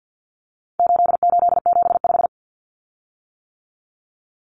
Morse code, heard as a keyed tone, about 50 wpm, 700 Hz. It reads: 8875
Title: Common Cut Numbers - 4-Digit 50wpm